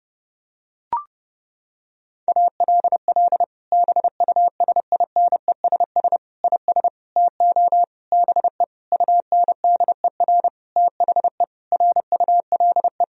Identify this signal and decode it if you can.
Morse code, heard as a keyed tone, about 30 wpm, 700 Hz. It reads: ALL BUSINESS IS TO BE UNDER THE RULE
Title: Sentences from Top 1000 Words - Review Part 3 - 30wpm